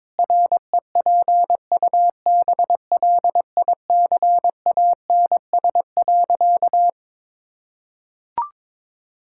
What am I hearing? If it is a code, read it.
Morse code, heard as a keyed tone, about 22 wpm, 700 Hz. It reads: REPUBLICANS.
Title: Pangram Challenge v2 - 22wpm